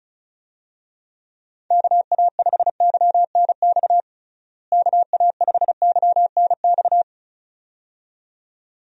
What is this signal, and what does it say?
Morse code, heard as a keyed tone, about 34 wpm, 700 Hz. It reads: KA5YNX KA5YNX
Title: USA Call Signs - 6 Characters - Repeated 2x - 35wpm